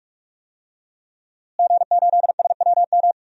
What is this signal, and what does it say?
Morse code, heard as a keyed tone, about 46 wpm, 700 Hz. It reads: G8SWM
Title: DXCC-100 Call Signs - 5 Characters - Repeated 3x - 45wpm